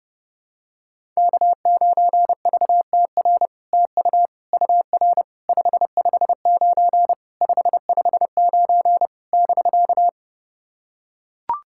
Morse code, heard as a keyed tone, about 30 wpm, 700 Hz. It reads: K9VTR TU UR 559 559 <BK>
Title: SOTA QSO Practice 30wpm